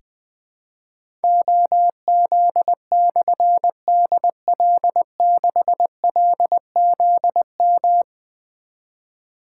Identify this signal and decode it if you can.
Morse code, heard as a keyed tone, about 20 wpm, 700 Hz. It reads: OZ/DL6LZM
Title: Difficult Call Signs 20wpm